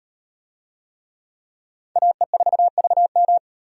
Morse code, heard as a keyed tone, about 37 wpm, 700 Hz. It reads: AE4VM